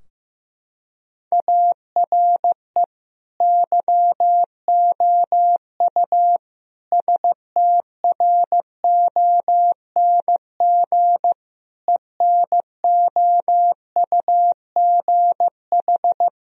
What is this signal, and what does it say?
Morse code, heard as a keyed tone, about 15 wpm, 700 Hz. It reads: ARE YOU STRONG ENOUGH
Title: Sentences from Top 1000 Words - Review Part 3 - 15wpm